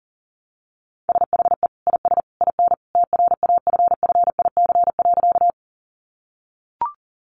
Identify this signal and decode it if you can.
Morse code, heard as a keyed tone, about 40 wpm, 700 Hz. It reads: SHE IS IN TRAFFIC.
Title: Sentences from Top 1300 Words 40wpm